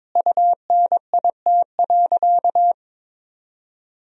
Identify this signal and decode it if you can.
Morse code, heard as a keyed tone, about 22 wpm, 700 Hz. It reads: UNIT.
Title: Sentences from Top 1000 Words - Review Part 2 - 22wpm